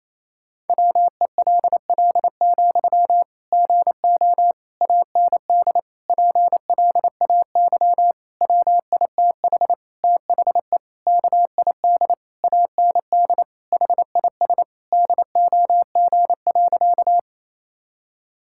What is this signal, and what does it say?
Morse code, heard as a keyed tone, about 28 wpm, 700 Hz. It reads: WELL, GO AND PLAY WITH THE KID AND HIS DOG.
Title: Sentences from Top 1600 Words 28wpm